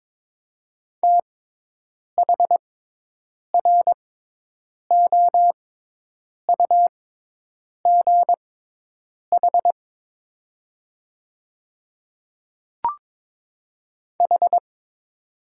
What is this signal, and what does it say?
Morse code, heard as a keyed tone, about 23 wpm, spaced as for 8 wpm, 700 Hz. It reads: THROUGH H
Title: 6x Character Spacing - Top 200 Words - 22wpm